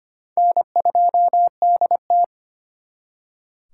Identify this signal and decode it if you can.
Morse code, heard as a keyed tone, about 33 wpm, 700 Hz. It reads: N2DT